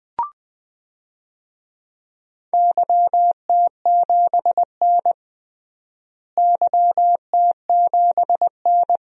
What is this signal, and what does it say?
Morse code, heard as a keyed tone, about 21 wpm, 700 Hz. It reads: YT7N YT7N
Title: DXCC-100 Call Signs - 4 Characters - Repeated 3x - 20wpm